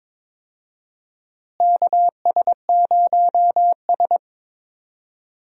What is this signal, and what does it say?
Morse code, heard as a keyed tone, about 21 wpm, 700 Hz. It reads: KS0S
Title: USA Call Signs - 4 Characters - Repeated 2x - 22wpm